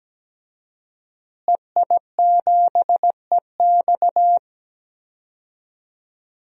Morse code, heard as a keyed tone, about 17 wpm, 700 Hz. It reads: EI7EX
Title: DXCC-100 Call Signs - 5 Characters - Repeated 3x - 17wpm